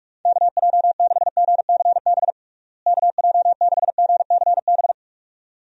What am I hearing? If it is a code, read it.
Morse code, heard as a keyed tone, about 47 wpm, 700 Hz. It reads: KJ6GCB KJ6GCB
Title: USA Call Signs - 6 Characters - Repeated 2x - 45wpm